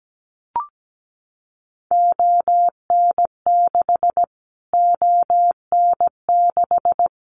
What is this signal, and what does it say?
Morse code, heard as a keyed tone, about 17 wpm, 700 Hz. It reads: ON6 ON6
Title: Callsign Prefixes for Top 25 DXCC - Repeated 3x - 17wpm